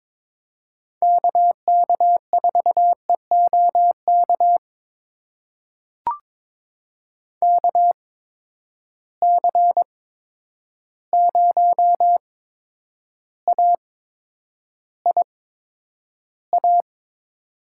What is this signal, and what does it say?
Morse code, heard as a keyed tone, about 23 wpm, spaced as for 10 wpm, 700 Hz. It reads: KK4EOK KC0AIA
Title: USA Call Signs - 6 Characters - 8x Character Spacing - 22wpm